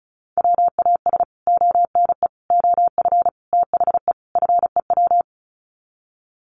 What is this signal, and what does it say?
Morse code, heard as a keyed tone, about 34 wpm, 700 Hz. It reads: WAS ONE OF THE FEW
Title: Sets of 5 Words Encore 35wpm